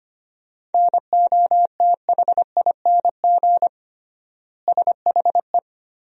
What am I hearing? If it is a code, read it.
Morse code, heard as a keyed tone, about 25 wpm, 700 Hz. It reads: NOTHING SHE
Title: 3x Word-Spacing - Sentences from Top 500 Words - 25wpm